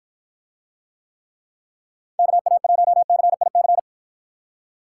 Morse code, heard as a keyed tone, about 54 wpm, 700 Hz. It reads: KA1CIC